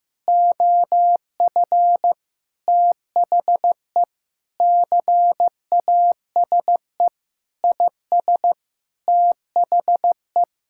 Morse code, heard as a keyed tone, about 14 wpm, 700 Hz. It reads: OF THE CASE IS THE
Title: Sentences from Top 1000 Words - Review Part 6 - 15wpm